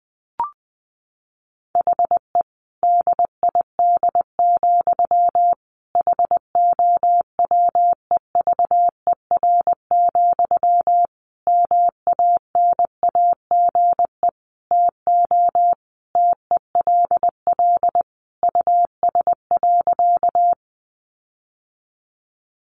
Morse code, heard as a keyed tone, about 20 wpm, 700 Hz. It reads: HE DID, HOWEVER, MANAGE TO TELL US.
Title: Sentences from Top 1400 Words 20wpm